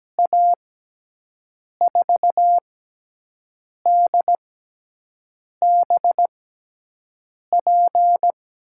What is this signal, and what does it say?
Morse code, heard as a keyed tone, about 17 wpm, spaced as for 6 wpm, 700 Hz. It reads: A4DBP
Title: USA Call Signs - 6x Character Spacing - 17wpm